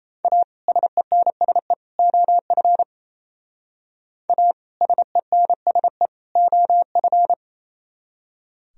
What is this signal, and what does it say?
Morse code, heard as a keyed tone, about 35 wpm, 700 Hz. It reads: A SENSE OF A SENSE OF
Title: Speed-Racing - Sets of 3 Words 28wpm